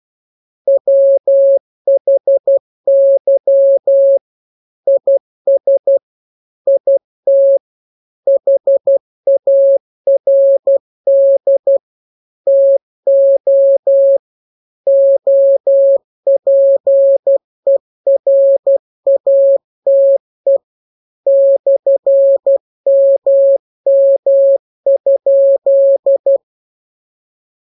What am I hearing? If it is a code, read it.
Morse code, heard as a keyed tone, about 12 wpm, 550 Hz. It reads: WHY IS IT HARD TO OPERATE /MM?